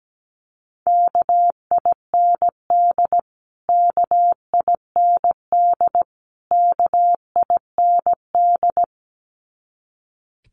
Morse code, heard as a keyed tone, about 17 wpm, 700 Hz. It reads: KIND KIND KIND
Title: Top 300 Words Repeated 17wpm